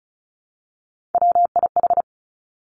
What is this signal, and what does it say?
Morse code, heard as a keyed tone, about 36 wpm, 700 Hz. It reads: WIH
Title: Three Challenging Letter-Number Combinations 35wpm